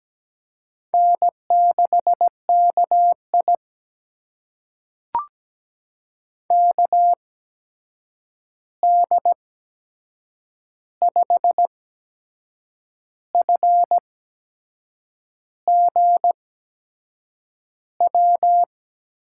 Morse code, heard as a keyed tone, about 17 wpm, spaced as for 6 wpm, 700 Hz. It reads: N6KI KD5FGW
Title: USA Call Signs - 8x Character Spacing - 17wpm